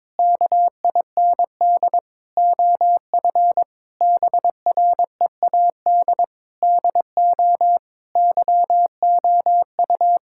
Morse code, heard as a keyed tone, about 22 wpm, 700 Hz. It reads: KIND OF BREAD DO YOU
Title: Sentences from Top 1600 Words 22wpm